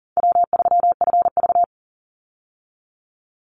Morse code, heard as a keyed tone, about 40 wpm, 700 Hz. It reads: W3FV
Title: Easy Call Signs Repeated 3x - 40wpm